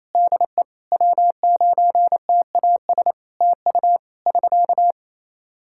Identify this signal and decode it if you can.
Morse code, heard as a keyed tone, about 27 wpm, 700 Hz. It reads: DE W9TAS TU <SK>
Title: K1USN SST QSO Practice 28wpm